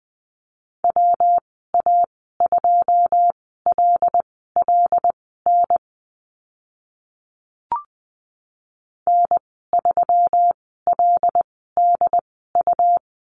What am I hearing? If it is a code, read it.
Morse code, heard as a keyed tone, about 20 wpm, spaced as for 15 wpm, 700 Hz. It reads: WA2LLN N3LDU
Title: 122 Lesson 20g - Number 3 - Call Signs 20wpm